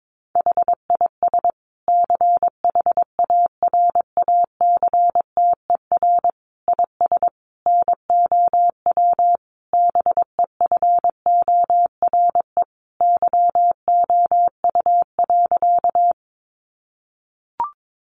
Morse code, heard as a keyed tone, about 22 wpm, 700 Hz. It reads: HIS CHARACTER IS NOW BEFORE YOU.
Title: Sentences from Top 1200 Words 22wpm